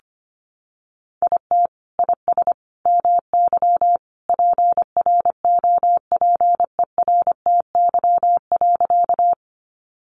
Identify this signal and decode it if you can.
Morse code, heard as a keyed tone, about 25 wpm, 700 Hz. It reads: IT IS MY PROPERTY.
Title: Sentences from Top 1000 Words 25wpm